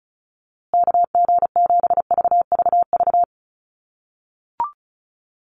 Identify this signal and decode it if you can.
Morse code, heard as a keyed tone, about 34 wpm, 700 Hz. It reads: KG7VVV
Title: USA Call Signs Repeated 3x 35wpm